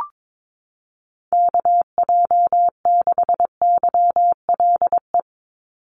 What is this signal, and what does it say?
Morse code, heard as a keyed tone, about 22 wpm, 700 Hz. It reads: KJ6YLE
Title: USA Call Signs 22wpm